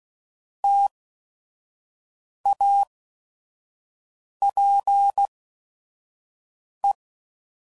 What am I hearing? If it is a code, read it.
Morse code, heard as a keyed tone, about 16 wpm, spaced as for 5 wpm, 800 Hz. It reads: TAPE
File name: Prefixes / By First Letter / Ts